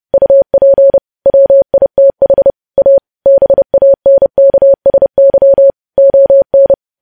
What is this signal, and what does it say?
Morse code, heard as a keyed tone, about 30 wpm, 550 Hz. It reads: UP WITH A BANKSY ON
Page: news headlines in morse code at 30 wpm